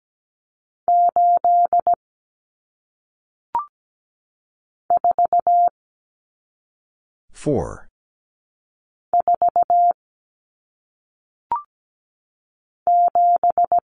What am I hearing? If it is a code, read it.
Morse code, heard as a keyed tone, about 17 wpm, 700 Hz. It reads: 8 4 4 7
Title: Common Cut Numbers - 1-Digit 17wpm